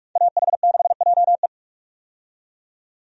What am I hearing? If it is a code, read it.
Morse code, heard as a keyed tone, about 46 wpm, 700 Hz. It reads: AH6JE
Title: USA Call Signs - 5 Characters - 45wpm